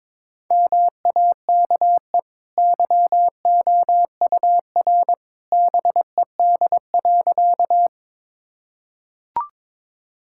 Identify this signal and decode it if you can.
Morse code, heard as a keyed tone, about 22 wpm, 700 Hz. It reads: MAKE YOUR BED.
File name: practice / 22wpm / Easy Sentences with Hints - 22wpm